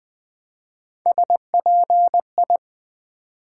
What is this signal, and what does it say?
Morse code, heard as a keyed tone, about 20 wpm, 700 Hz. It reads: SPI
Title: Build A Word - 1 Letter At A Time - 5-Letter Words - 20wpm